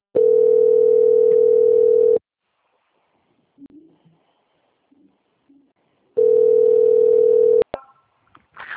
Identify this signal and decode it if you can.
Ringback tone (North America), 440 + 480 Hz.
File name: out- - - - - .17410